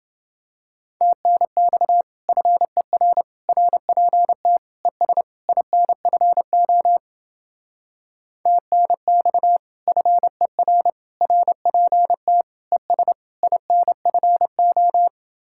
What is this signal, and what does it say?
Morse code, heard as a keyed tone, about 28 wpm, 700 Hz. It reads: TNX FER RPT ES INFO TNX FER RPT ES INFO